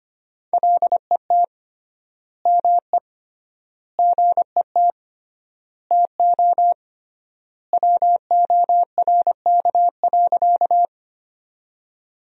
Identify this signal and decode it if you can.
Morse code, heard as a keyed tone, about 25 wpm, 700 Hz. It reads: LET ME GET TO WORK.
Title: 3x Word-Spacing - Sentences from Top 300 Words - 25wpm